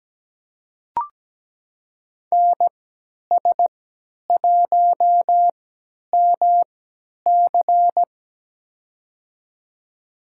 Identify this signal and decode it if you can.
Morse code, heard as a keyed tone, about 17 wpm, spaced as for 10 wpm, 700 Hz. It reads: NS1MC
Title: USA Call Signs - 5 Characters - 3x Character Spacing - 17wpm